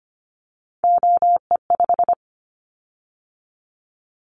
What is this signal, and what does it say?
Morse code, heard as a keyed tone, about 25 wpm, 700 Hz. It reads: OE5
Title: Callsign Prefixes for Top 100 DXCC - 25wpm